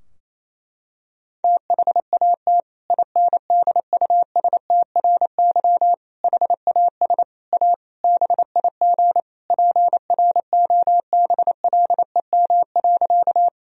Morse code, heard as a keyed tone, about 28 wpm, 700 Hz. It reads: THAT INDUSTRY HAS A BIG PROBLEM.
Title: Sentences from Top 1000 Words - Review Part 4 - 28wpm